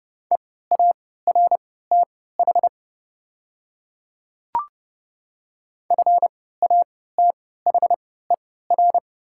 Morse code, heard as a keyed tone, about 30 wpm, spaced as for 17 wpm, 690 Hz. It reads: EARTH FATHER
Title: 3x Character Spacing - Top 200 Words - 30wpm